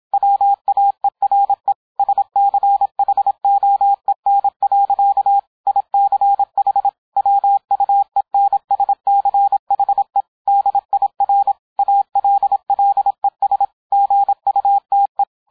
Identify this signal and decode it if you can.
Morse code, heard as a keyed tone, about 27 wpm, 800 Hz. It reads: WAERE SCHOEN. ICH WUENSCHE DIR ALLES GUTE